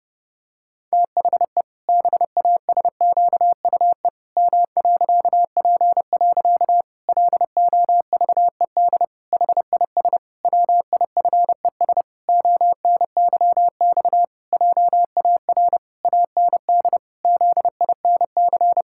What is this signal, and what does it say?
Morse code, heard as a keyed tone, about 30 wpm, 700 Hz. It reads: THE BASQUE M.P. LOVED HIS WIFES ONYX JAR AND ZINC
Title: Pangram Challenge v2 - 30wpm